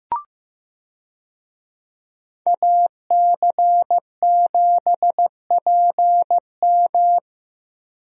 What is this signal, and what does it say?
Morse code, heard as a keyed tone, about 15 wpm, 700 Hz. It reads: AC7PM
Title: USA Call Signs - 5 Characters - Repeated 2x - 15wpm